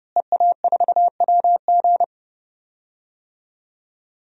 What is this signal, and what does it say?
Morse code, heard as a keyed tone, about 30 wpm, 700 Hz. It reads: EA4WG
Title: DXCC-100 Call Signs - 5 Characters - Repeated 3x - 30wpm